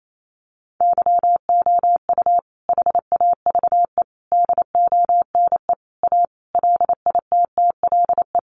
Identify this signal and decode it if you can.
Morse code, heard as a keyed tone, about 28 wpm, 700 Hz. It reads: YOU HAVE DONE A LITTLE
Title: Sentences from Top 700 Words 28wpm